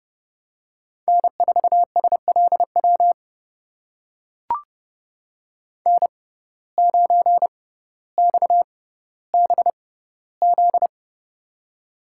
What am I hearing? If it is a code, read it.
Morse code, heard as a keyed tone, about 30 wpm, spaced as for 15 wpm, 700 Hz. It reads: N4SLW N9XBZ
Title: USA Call Signs - 6x Character Spacing - 30wpm